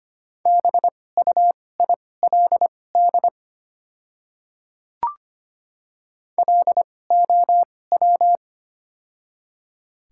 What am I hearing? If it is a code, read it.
Morse code, heard as a keyed tone, about 25 wpm, spaced as for 18 wpm, 700 Hz. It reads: BUILD LOW
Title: 2x Character Spacing - Top 200 Words - 25wpm